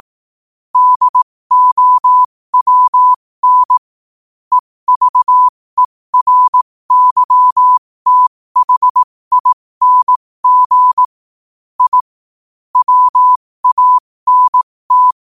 Morse code, heard as a keyed tone, about 18 wpm, spaced as for 15 wpm, 1000 Hz. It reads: DOWN EVERYTHING I WANT
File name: qotdfiles / 15wpm